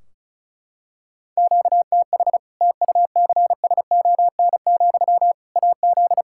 Morse code, heard as a keyed tone, about 34 wpm, 700 Hz. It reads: QTH TUCSON, AZ
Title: Practice QSO - Exchange 1 - Beginning - 35wpm